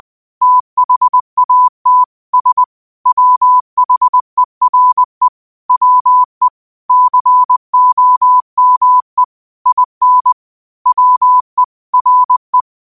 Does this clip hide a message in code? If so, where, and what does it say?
Yes: Morse code from 0.4 to 12.6 s, heard as a keyed tone, about 20 wpm, 1000 Hz. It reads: THAT S WHERE WE COME IN WE RE